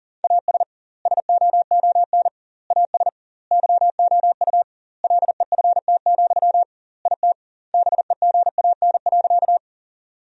Morse code, heard as a keyed tone, about 37 wpm, 700 Hz. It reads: AS SOON AS YOU LEFT, IT BEGAN.
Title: 2x Word-Spacing - Sentences from Top 400 Words - 40wpm